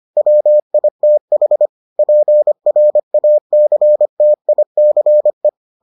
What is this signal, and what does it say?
Morse code, heard as a keyed tone, about 25 wpm, 600 Hz. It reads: WITH PRACTICE